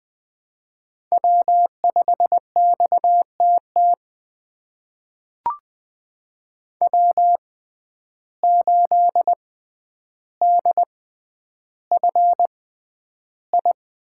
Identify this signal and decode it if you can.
Morse code, heard as a keyed tone, about 20 wpm, spaced as for 10 wpm, 700 Hz. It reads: W5XTT W8DFI
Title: USA Call Signs - 5 Characters - 6x Character Spacing - 20wpm